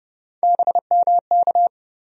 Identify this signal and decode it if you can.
Morse code, heard as a keyed tone, about 30 wpm, 700 Hz. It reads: BMK
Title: USA Call Sign Postfixes - 3 Letters 30wpm